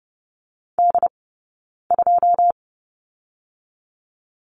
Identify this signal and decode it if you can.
Morse code, heard as a keyed tone, about 30 wpm, 700 Hz. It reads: D 2
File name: 092 Lesson 16d - Letter c - Two Characters 30wpm